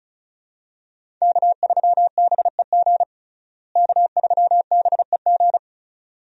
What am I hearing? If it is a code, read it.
Morse code, heard as a keyed tone, about 34 wpm, 700 Hz. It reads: K3BEG K3BEG
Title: USA Call Signs - 5 Characters - Repeated 2x - 35wpm